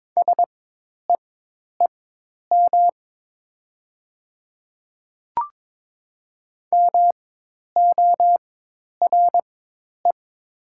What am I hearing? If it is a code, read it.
Morse code, heard as a keyed tone, about 22 wpm, spaced as for 10 wpm, 700 Hz. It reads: SEEM MORE
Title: 4x Character Spacing - Top 300 Words - 22wpm